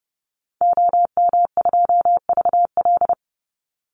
Morse code, heard as a keyed tone, about 30 wpm, 700 Hz. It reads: OM2VL
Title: Call Signs Repeated 3x - 30wpm